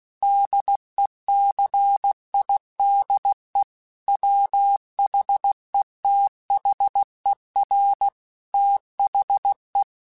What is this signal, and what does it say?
Morse code, heard as a keyed tone, about 16 wpm, 800 Hz. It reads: DECIDE WHETHER THE
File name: Chuckles / Chuckles-mp3s / Glass Half Full @15